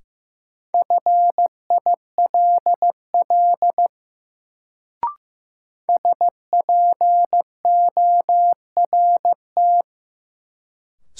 Morse code, heard as a keyed tone, about 16 wpm, 700 Hz. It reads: FILL SPORT
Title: Sentences from Top 1500 Words - New Word Drill Alt - 15wpm